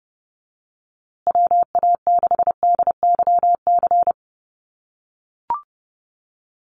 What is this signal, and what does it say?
Morse code, heard as a keyed tone, about 30 wpm, 700 Hz. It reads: WA6DYC
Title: USA Call Signs Repeated 3x 30wpm